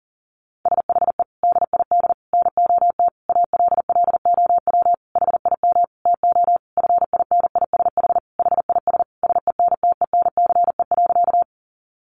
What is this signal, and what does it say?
Morse code, heard as a keyed tone, about 40 wpm, 700 Hz. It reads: SHE DID NOT ALLOW HIM TO FINISH HIS SENTENCE.
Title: Sentences from Top 1200 Words 40wpm